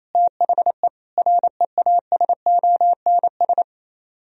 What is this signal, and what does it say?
Morse code, heard as a keyed tone, about 26 wpm, 700 Hz. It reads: THE REASONS
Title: Speed-Racing - Sets of 4 Words 28wpm